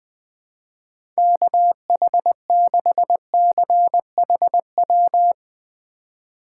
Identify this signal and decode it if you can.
Morse code, heard as a keyed tone, about 21 wpm, 700 Hz. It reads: KH6CHW
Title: USA Call Signs - 6 Characters - 3x Character Spacing - 20wpm